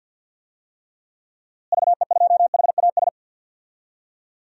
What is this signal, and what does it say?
Morse code, heard as a keyed tone, about 47 wpm, 700 Hz. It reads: VE2HAS
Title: DXCC-100 Call Signs - 6 Characters - Repeated 3x - 50wpm